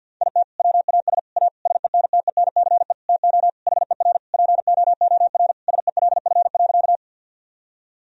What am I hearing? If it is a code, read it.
Morse code, heard as a keyed tone, about 53 wpm, 700 Hz. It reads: IT WAS A SENTENCE TO HER POOR SELF.
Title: Sentences from Top 1000 Words - Review Part 1 - 50wpm